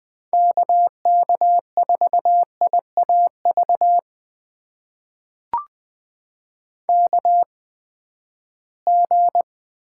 Morse code, heard as a keyed tone, about 20 wpm, spaced as for 14 wpm, 700 Hz. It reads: KK4IAV KG
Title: USA Call Signs - 6 Characters - 8x Character Spacing - 20wpm